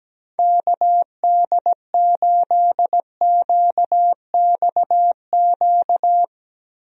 Morse code, heard as a keyed tone, about 17 wpm, 700 Hz. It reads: KD8QXQ
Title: USA Call Signs - 6 Characters - 17wpm